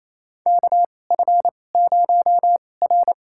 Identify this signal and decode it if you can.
Morse code, heard as a keyed tone, about 28 wpm, spaced as for 20 wpm, 700 Hz. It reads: KF0R
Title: USA Call Signs - 4 Characters - 2x Character Spacing - 28wpm